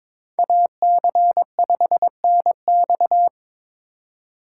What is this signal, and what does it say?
Morse code, heard as a keyed tone, about 26 wpm, 700 Hz. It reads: AC5NX
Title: Speed Racing - USA 2x2 Call Signs 22wpm